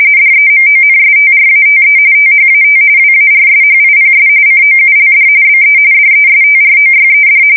RTTY, about 45.45 baud. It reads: DEMONSTRATE THAT WARSHIPS COULD BE DESTROYED
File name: fsk170 / usnavy4